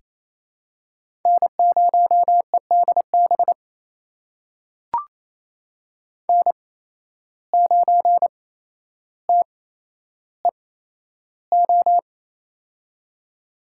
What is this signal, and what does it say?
Morse code, heard as a keyed tone, about 28 wpm, spaced as for 12 wpm, 700 Hz. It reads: N0EDB N9TEO
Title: USA Call Signs - 5 Characters - 8x Character Spacing - 28wpm